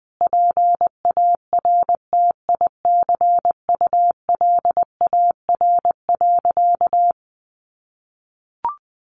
Morse code, heard as a keyed tone, about 20 wpm, 700 Hz. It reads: PARTICULAR.
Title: Sentences from Top 600 Words 20wpm